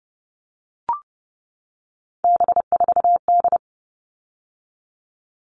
Morse code, heard as a keyed tone, about 29 wpm, 700 Hz. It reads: B4D
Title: All Cut Numbers - 3-Digit 30wpm